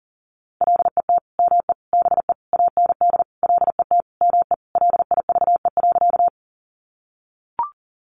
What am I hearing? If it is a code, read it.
Morse code, heard as a keyed tone, about 40 wpm, 700 Hz. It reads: LET ME BE AND LET ME LIVE.
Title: Sentences from Top 300 Words 40wpm